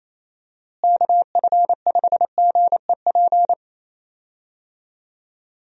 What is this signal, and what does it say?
Morse code, heard as a keyed tone, about 28 wpm, 700 Hz. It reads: KF5GEP
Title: USA Call Signs - 6 Characters - Repeated 2x - 28wpm